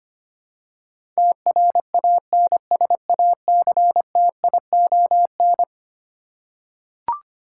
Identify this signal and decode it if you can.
Morse code, heard as a keyed tone, about 25 wpm, 700 Hz. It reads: TRANSACTION
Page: morse-code-ninja-practice-25wpm